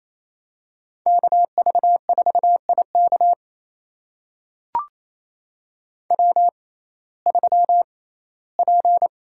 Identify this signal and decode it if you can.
Morse code, heard as a keyed tone, about 28 wpm, spaced as for 17 wpm, 700 Hz. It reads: KV4IK W3P
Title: USA Call Signs - 5 Characters - 6x Character Spacing - 28wpm